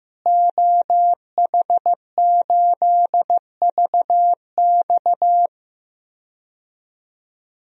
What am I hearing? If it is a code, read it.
Morse code, heard as a keyed tone, about 15 wpm, 700 Hz. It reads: OH8VX
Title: DXCC-100 Call Signs - 5 Characters - Repeated 2x - 15wpm